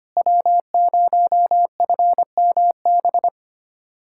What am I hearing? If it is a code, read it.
Morse code, heard as a keyed tone, about 25 wpm, 700 Hz. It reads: W0FMB
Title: USA Callsign DE Callsign - 5 Characters - 25wpm